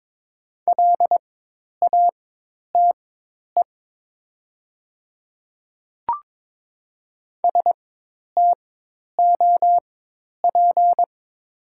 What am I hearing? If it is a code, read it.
Morse code, heard as a keyed tone, about 22 wpm, spaced as for 10 wpm, 700 Hz. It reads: LATE STOP
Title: 4x Character Spacing - Top 300 Words - 22wpm